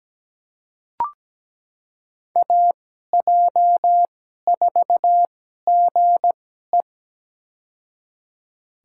Morse code, heard as a keyed tone, about 17 wpm, spaced as for 12 wpm, 700 Hz. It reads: AJ4GE
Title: USA Call Signs - 2x Character Spacing - 17wpm